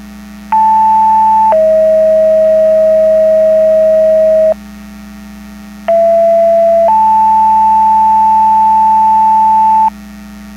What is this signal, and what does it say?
Two-tone page, heard as a one-second tone, then a three-second tone, about 880 then 640 Hz, about 670 then 880 Hz. It is heard from 0.5 to 9.9 s.
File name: Portage 154.385 Paging